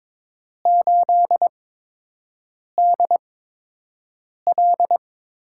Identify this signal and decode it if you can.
Morse code, heard as a keyed tone, about 22 wpm, spaced as for 6 wpm, 700 Hz. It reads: 8DL
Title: USA Call Signs - 8x Character Spacing - 22wpm